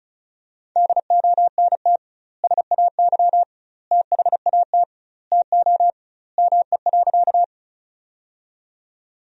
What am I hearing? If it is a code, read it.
Morse code, heard as a keyed tone, about 36 wpm, 700 Hz. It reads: DONT SAY THAT TO ME.
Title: 2x Word-Spacing - Sentences from Top 300 Words - 35wpm